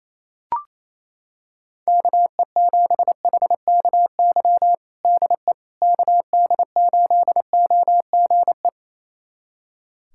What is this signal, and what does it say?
Morse code, heard as a keyed tone, about 28 wpm, 700 Hz. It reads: KE7HKY DE KD8OGE
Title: USA Callsign DE Callsign - 6 Characters - 28wpm